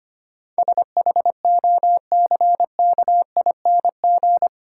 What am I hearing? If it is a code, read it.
Morse code, heard as a keyed tone, about 25 wpm, 700 Hz. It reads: SHOCKING
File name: Eight Letter Words - 25wpm